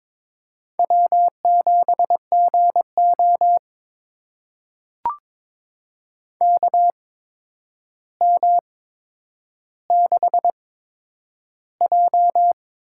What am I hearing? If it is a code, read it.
Morse code, heard as a keyed tone, about 22 wpm, spaced as for 9 wpm, 700 Hz. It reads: W7GO KM6J
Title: USA Call Signs - 8x Character Spacing - 22wpm